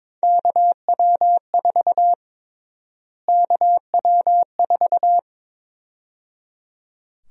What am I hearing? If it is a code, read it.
Morse code, heard as a keyed tone, about 22 wpm, 700 Hz. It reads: KW4 KW4
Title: USA Call Sign Prefixes - Repeated 2x - 22wpm